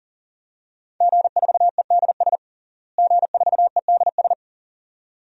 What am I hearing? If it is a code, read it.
Morse code, heard as a keyed tone, about 40 wpm, 700 Hz. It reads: G4EDS G4EDS
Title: DXCC-100 Call Signs - 5 Characters - Repeated 2x - 40wpm